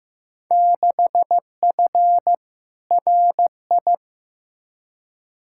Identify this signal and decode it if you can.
Morse code, heard as a keyed tone, about 15 wpm, 700 Hz. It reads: 6F RI
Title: Field Day Exchanges Encore - 15wpm